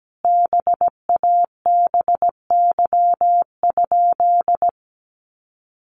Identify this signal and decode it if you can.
Morse code, heard as a keyed tone, about 17 wpm, 700 Hz. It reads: BABY?
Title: Sentences from Top 1300 Words 17wpm